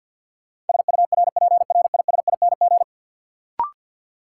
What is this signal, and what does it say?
Morse code, heard as a keyed tone, about 47 wpm, 700 Hz. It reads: SURPRISING
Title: Sets of 5 Words - New Word Drill - 50wpm